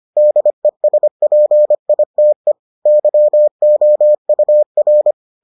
Morse code, heard as a keyed tone, about 25 wpm, 600 Hz. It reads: DESPITE YOUR